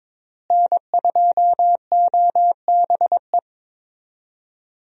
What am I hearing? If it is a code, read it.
Morse code, heard as a keyed tone, about 22 wpm, 700 Hz. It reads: N2OBE
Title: USA Call Signs - 5 Characters - Repeated 3x - 22wpm